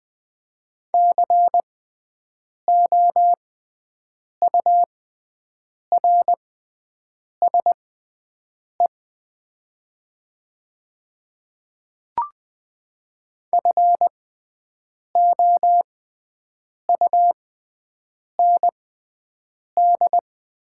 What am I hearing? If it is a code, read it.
Morse code, heard as a keyed tone, about 20 wpm, spaced as for 7 wpm, 700 Hz. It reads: COURSE FOUND
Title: 6x Character Spacing - Top 500 Words - 20wpm